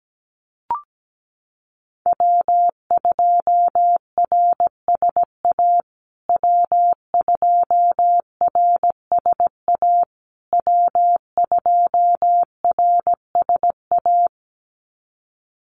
Morse code, heard as a keyed tone, about 17 wpm, 700 Hz. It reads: W2RSA W2RSA W2RSA
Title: USA Call Signs Repeated 3x 17wpm